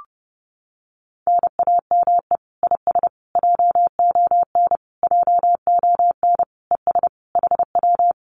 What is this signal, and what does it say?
Morse code, heard as a keyed tone, about 30 wpm, 700 Hz. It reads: NAME IS JON JON ES HW
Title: Common QSO Phrases Encore 30wpm